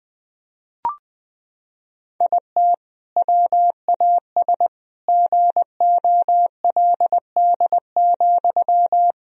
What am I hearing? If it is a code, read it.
Morse code, heard as a keyed tone, about 20 wpm, 700 Hz. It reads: IT WAS GOLD,
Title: Sentences from Top 1000 Words - Review Part 3 - 20wpm